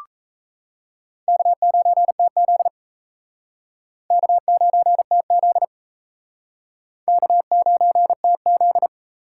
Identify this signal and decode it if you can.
Morse code, heard as a keyed tone, about 36 wpm, 700 Hz. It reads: K9TZ K9TZ K9TZ